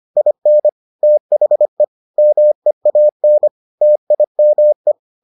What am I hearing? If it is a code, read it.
Morse code, heard as a keyed tone, about 25 wpm, 600 Hz. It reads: IN THE MEAN TIME